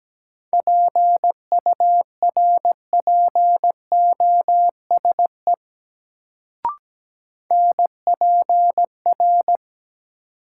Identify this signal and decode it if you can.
Morse code, heard as a keyed tone, about 17 wpm, 700 Hz. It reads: PURPOSE NPR
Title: Sets of 5 Words - New Word Drill - 17wpm